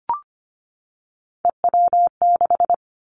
Morse code, heard as a keyed tone, about 25 wpm, 700 Hz. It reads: EW6
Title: Callsign Prefixes for Top 50 DXCC - 25wpm